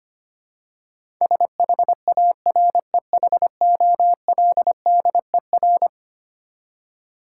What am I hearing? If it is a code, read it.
Morse code, heard as a keyed tone, about 25 wpm, 700 Hz. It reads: SHAREHOLDER
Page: morse-code-ninja-practice-25wpm